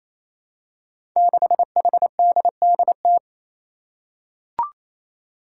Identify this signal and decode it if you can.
Morse code, heard as a keyed tone, about 27 wpm, 700 Hz. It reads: 6HDDT
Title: Five Challenging Letter-Number Combinations 28wpm